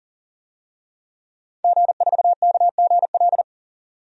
Morse code, heard as a keyed tone, about 40 wpm, 700 Hz. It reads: G4KGL
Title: DXCC-100 Call Signs - 5 Characters - 40wpm